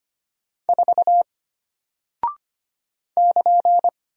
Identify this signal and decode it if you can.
Morse code, heard as a keyed tone, about 25 wpm, 700 Hz. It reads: 4 <KN>
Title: Single Character (Letter:Number:Punctuation:Prosign) Alt 25wpm